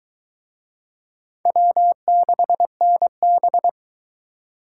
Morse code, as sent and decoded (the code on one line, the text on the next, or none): .-- -.... -. -...
W6NB